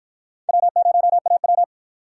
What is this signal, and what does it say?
Morse code, heard as a keyed tone, about 53 wpm, 700 Hz. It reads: W0AW